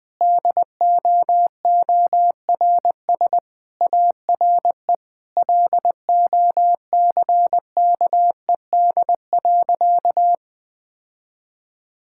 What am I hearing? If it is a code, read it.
Morse code, heard as a keyed tone, about 19 wpm, 700 Hz. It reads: DOORS ARE LOCKED.
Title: Sentences with 5-Word Phrases - Spoken Beginning - 20wpm